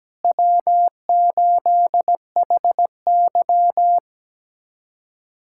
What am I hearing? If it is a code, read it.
Morse code, heard as a keyed tone, about 17 wpm, 700 Hz. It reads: W8HY
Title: USA Callsign DE Callsign - 4 Characters - 17wpm